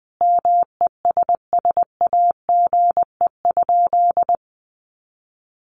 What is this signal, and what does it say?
Morse code, heard as a keyed tone, about 20 wpm, 700 Hz. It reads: MESSAGE?
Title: Sentences from Top 1100 Words 20wpm